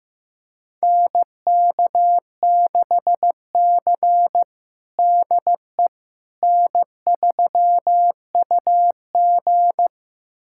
Morse code, heard as a keyed tone, about 15 wpm, 700 Hz. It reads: NK6C DE N3UG
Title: USA Callsign DE Callsign - 4 Characters - 15wpm